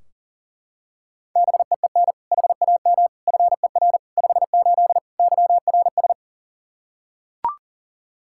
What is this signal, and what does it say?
Morse code, heard as a keyed tone, about 40 wpm, 700 Hz. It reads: BEEN HAM FER 58 YRS
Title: Practice QSO - Exchange 3 - More Info - 40wpm